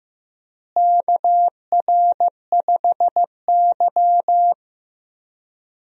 Morse code, heard as a keyed tone, about 15 wpm, 700 Hz. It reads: KR5Y
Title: USA Call Signs - 4 Characters - Repeated 2x - 15wpm